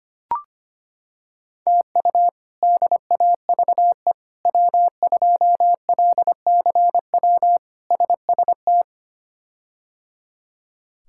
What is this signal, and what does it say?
Morse code, heard as a keyed tone, about 25 wpm, 700 Hz. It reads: TU DAVE W2LCW SST
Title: K1USN SST QSO Practice - 25wpm